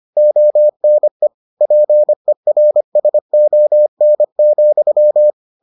Morse code, heard as a keyed tone, about 25 wpm, 600 Hz. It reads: ONE PERSON,